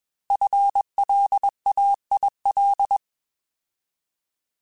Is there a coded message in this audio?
Yes: Morse code from 0.3 to 3.0 s, heard as a keyed tone, about 22 wpm, 800 Hz. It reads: FLAIL